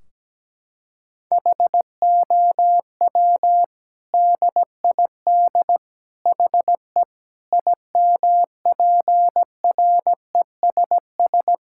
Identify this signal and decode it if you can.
Morse code, heard as a keyed tone, about 17 wpm, 700 Hz. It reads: HOW DID HE IMPRESS
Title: Sentences from Top 1600 Words 17wpm